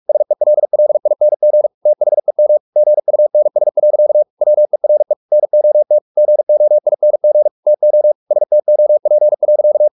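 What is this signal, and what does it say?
Morse code, heard as a keyed tone, about 44 wpm, 600 Hz. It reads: SELLING THEM GUNS. WERE NOT GOING TO STOP.
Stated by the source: Morse practice at 45 wpm, the novel Never by Ken Follett in English, from CW med Gnister.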